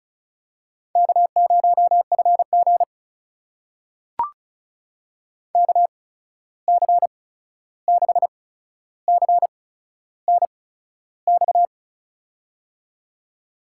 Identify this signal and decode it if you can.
Morse code, heard as a keyed tone, about 36 wpm, spaced as for 13 wpm, 700 Hz. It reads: K0FG KC6CNX
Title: USA Call Signs - 8x Character Spacing - 35wpm